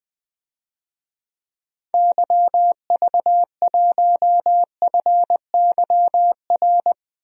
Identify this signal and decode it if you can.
Morse code, heard as a keyed tone, about 20 wpm, 700 Hz. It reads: YV1FYR
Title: DXCC-100 Call Signs - 6 Characters - 20wpm